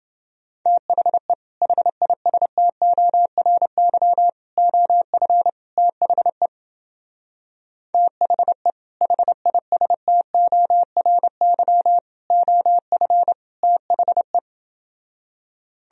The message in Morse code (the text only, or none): THE HISTORY OF THE THE HISTORY OF THE